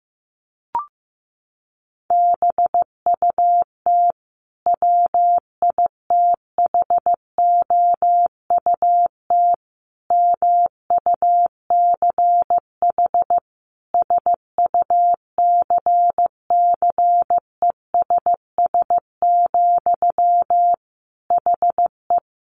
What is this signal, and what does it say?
Morse code, heard as a keyed tone, about 14 wpm, 700 Hz. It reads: BUT WITHOUT MUCH SUCCESS, HE
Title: Sentences from Top 900 Words 15wpm